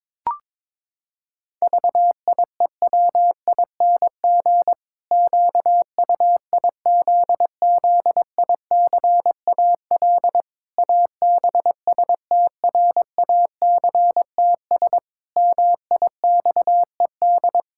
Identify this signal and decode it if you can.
Morse code, heard as a keyed tone, about 22 wpm, 700 Hz. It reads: VIEWING QUIZZICAL ABSTRACTS MIXED
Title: Pangram Challenge v2 - 22wpm